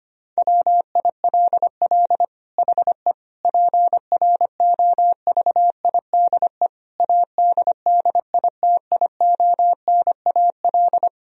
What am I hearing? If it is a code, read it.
Morse code, heard as a keyed tone, about 25 wpm, 700 Hz. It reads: WILL HE PROVIDE ADDITIONAL
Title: Sentences from Top 1000 Words - Review Part 4 - 25wpm